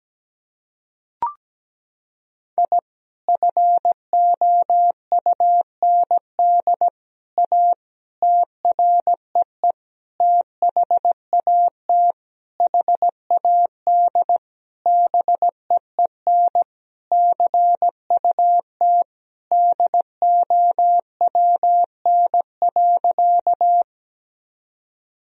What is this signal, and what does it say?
Morse code, heard as a keyed tone, about 17 wpm, 700 Hz. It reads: I FOUND A TREE THAT HAD BEEN CUT DOWN.
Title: Sentences from Top 1000 Words - Review Part 1 - 17wpm